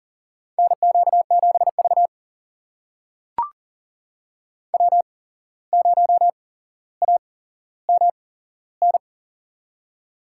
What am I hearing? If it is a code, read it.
Morse code, heard as a keyed tone, about 40 wpm, spaced as for 16 wpm, 700 Hz. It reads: NQ7V W0AMN
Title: USA Call Signs - 8x Character Spacing - 40wpm